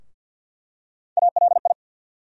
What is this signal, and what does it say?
Morse code, heard as a keyed tone, about 49 wpm, 700 Hz. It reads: ALI
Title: USA Call Sign Postfixes - 3 Letters 50wpm